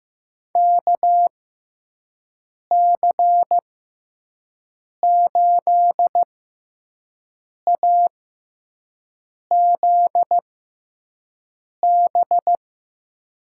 Morse code, heard as a keyed tone, about 15 wpm, spaced as for 5 wpm, 700 Hz. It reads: KC8AZB
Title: USA Call Signs - 6x Character Spacing - 15wpm